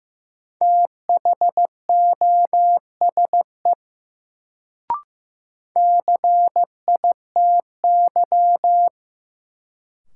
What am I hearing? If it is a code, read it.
Morse code, heard as a keyed tone, about 15 wpm, 700 Hz. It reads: THOSE CITY
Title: Sentences from Top 300 Words - New Word Drill - 15wpm